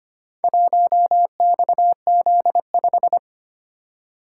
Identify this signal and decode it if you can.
Morse code, heard as a keyed tone, about 25 wpm, 700 Hz. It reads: 1XZ5
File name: practice / 25wpm / Four Letter-Number Combinations - 25wpm